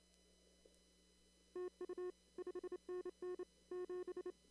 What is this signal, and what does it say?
Morse code, heard as a keyed tone, about 27 wpm, 360 Hz. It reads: TU 5NN 7
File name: CW / kt5j wpx cw-op-k5pi